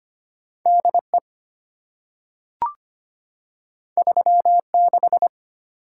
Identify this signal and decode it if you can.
Morse code, heard as a keyed tone, about 25 wpm, 690 Hz. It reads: DE 36
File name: Two Letter-Number Combinations - 25wpm